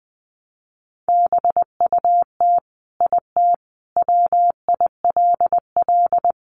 Morse code, heard as a keyed tone, about 20 wpm, 700 Hz. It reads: BUT IT WILL
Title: Sentences from Top 1300 Words 20wpm